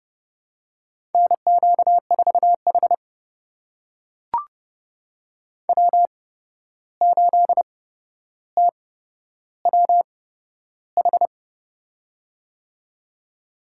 Morse code, heard as a keyed tone, about 30 wpm, spaced as for 12 wpm, 700 Hz. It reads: NQ4H W8TWH
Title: USA Call Signs - 8x Character Spacing - 30wpm